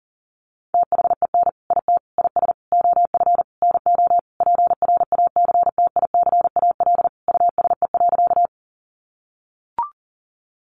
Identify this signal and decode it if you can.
Morse code, heard as a keyed tone, about 40 wpm, 700 Hz. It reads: THEN IT IS OF NO PRACTICAL USE.
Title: Sentences from Top 1400 Words 40wpm